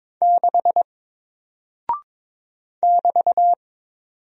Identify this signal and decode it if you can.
Morse code, heard as a keyed tone, about 22 wpm, 700 Hz. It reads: 6 <BT>
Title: Single Character (Letter:Number:Punctuation:Prosign) Alt 22wpm